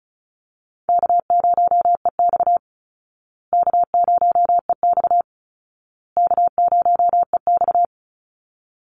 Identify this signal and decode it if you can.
Morse code, heard as a keyed tone, about 34 wpm, 700 Hz. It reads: K0EX K0EX K0EX
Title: Call Signs Repeated 2x - 35wpm